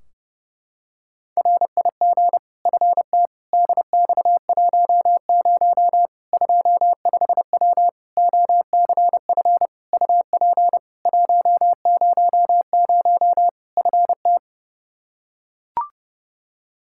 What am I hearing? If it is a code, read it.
Morse code, heard as a keyed tone, about 32 wpm, 700 Hz. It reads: RIG FT DX10 25W OCF UP 100 FT
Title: Practice QSO - Exchange 2 - Info - 30wpm